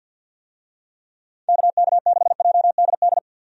Morse code, heard as a keyed tone, about 47 wpm, 700 Hz. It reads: KK6JDD
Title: USA Call Signs - 6 Characters - 50wpm